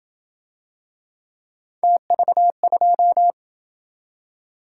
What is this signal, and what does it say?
Morse code, heard as a keyed tone, about 27 wpm, 700 Hz. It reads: TV2